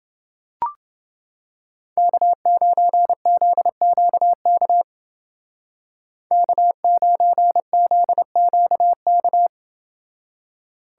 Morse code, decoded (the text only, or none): K9ZQK K9ZQK